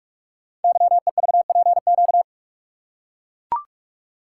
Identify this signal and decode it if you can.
Morse code, heard as a keyed tone, about 44 wpm, 700 Hz. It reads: YEVPQ
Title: Code Groups - 5 Letters - 45wpm